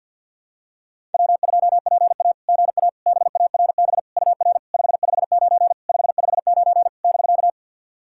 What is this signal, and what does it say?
Morse code, heard as a keyed tone, about 49 wpm, 700 Hz. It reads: W2PA GA BARB UR 559 559 <BK>
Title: SOTA QSO Practice 50wpm